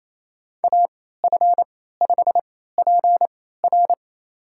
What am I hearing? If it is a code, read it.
Morse code, heard as a keyed tone, about 28 wpm, spaced as for 16 wpm, 700 Hz. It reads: AF5PR
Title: USA Call Signs - 3x Character Spacing - 28wpm